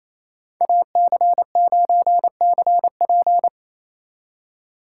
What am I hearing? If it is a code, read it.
Morse code, heard as a keyed tone, about 33 wpm, 700 Hz. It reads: AC9CP
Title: Speed Racing - USA Call Signs 28wpm